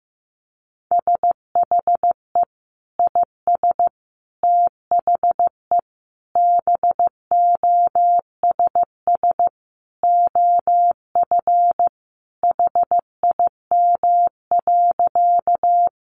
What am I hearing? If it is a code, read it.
Morse code, heard as a keyed tone, about 15 wpm, 700 Hz. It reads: SHE IS THE BOSS OF HIM.
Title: Sentences from Top 700 Words 15wpm